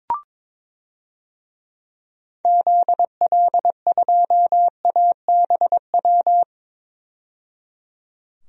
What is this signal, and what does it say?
Morse code, heard as a keyed tone, about 21 wpm, 700 Hz. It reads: ZL2ABW
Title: DXCC-100 Call Signs - 6 Characters - 22wpm